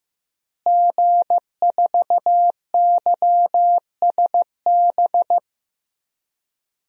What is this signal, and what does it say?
Morse code, heard as a keyed tone, about 16 wpm, 700 Hz. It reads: G4YSB
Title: DXCC-100 Call Signs - 5 Characters - Repeated 3x - 15wpm